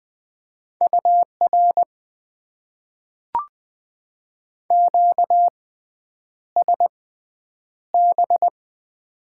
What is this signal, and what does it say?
Morse code, heard as a keyed tone, about 20 wpm, spaced as for 9 wpm, 700 Hz. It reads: UR QSB
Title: Top 90 QSO Elements - 6x Character Spacing - 20wpm